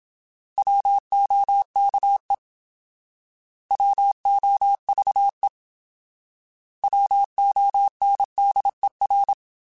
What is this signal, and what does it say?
Morse code, heard as a keyed tone, about 26 wpm, 800 Hz. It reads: WOKE WOVE WONDER
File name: Alphabetical / Alphabetical mp3s / Ws @25